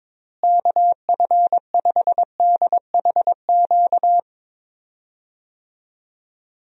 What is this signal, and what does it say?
Morse code, heard as a keyed tone, about 22 wpm, 700 Hz. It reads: KF5DHQ
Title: USA Call Signs - 6 Characters - Repeated 3x - 22wpm